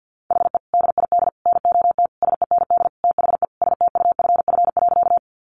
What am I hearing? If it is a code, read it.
Morse code, heard as a keyed tone, about 50 wpm, 700 Hz. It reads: HE DID NOT SEND THE STUFF.
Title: Sentences from Top 1500 Words 50wpm